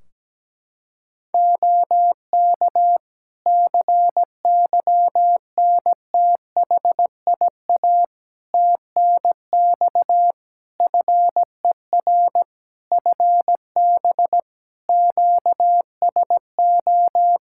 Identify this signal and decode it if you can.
Morse code, heard as a keyed tone, about 17 wpm, 700 Hz. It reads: OK CYNTHIA TNX FER FB QSO
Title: K1USN SST QSO Practice 17wpm